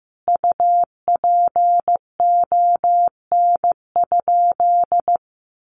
Morse code, heard as a keyed tone, about 14 wpm, 700 Hz. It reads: UPON?
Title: Sentences from Top 1100 Words 15wpm